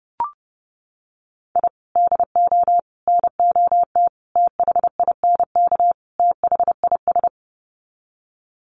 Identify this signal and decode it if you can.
Morse code, heard as a keyed tone, about 30 wpm, 700 Hz. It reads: I DO NOT THINK THIS
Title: Sets of 5 Words Encore 30wpm